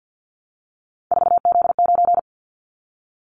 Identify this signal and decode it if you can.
Morse code, heard as a keyed tone, about 55 wpm, 700 Hz. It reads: 4Z9
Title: Callsign Prefixes for Top 50 DXCC - 50wpm